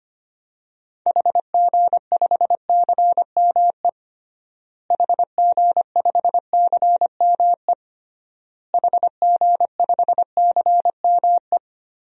Morse code, heard as a keyed tone, about 25 wpm, 700 Hz. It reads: HG5CME HG5CME HG5CME
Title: DXCC-100 Call Signs - 6 Characters - Repeated 3x - 25wpm